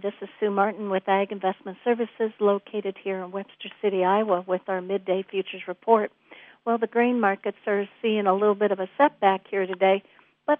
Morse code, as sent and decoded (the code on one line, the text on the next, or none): none